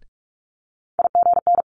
Morse code, heard as a keyed tone, about 46 wpm, 700 Hz. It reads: IGN
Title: Multi-letter Phonemes Quick Fire 45wpm